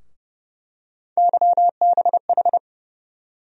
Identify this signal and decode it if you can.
Morse code, heard as a keyed tone, about 30 wpm, 700 Hz. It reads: YBH
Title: USA Call Sign Postfixes - 3 Letters 30wpm